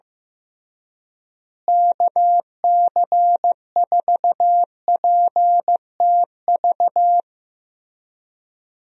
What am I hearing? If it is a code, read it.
Morse code, heard as a keyed tone, about 15 wpm, 700 Hz. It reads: KC4PTV
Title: USA Call Signs - 6 Characters - Repeated 2x - 15wpm